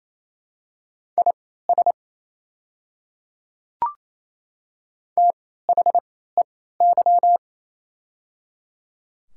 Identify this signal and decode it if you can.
Morse code, heard as a keyed tone, about 28 wpm, spaced as for 16 wpm, 700 Hz. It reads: IS THEY
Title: 3x Character Spacing - Top 500 Words - 28wpm